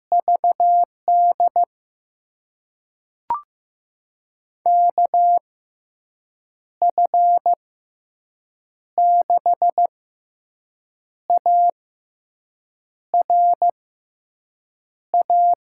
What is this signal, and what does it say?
Morse code, heard as a keyed tone, about 15 wpm, spaced as for 6 wpm, 700 Hz. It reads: VD KF6ARA
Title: USA Call Signs - 6 Characters - 6x Character Spacing - 15wpm